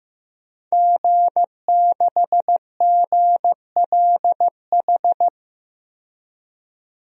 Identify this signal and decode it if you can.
Morse code, heard as a keyed tone, about 15 wpm, 700 Hz. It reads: G6GLH
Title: DXCC-100 Call Signs - 5 Characters - 15wpm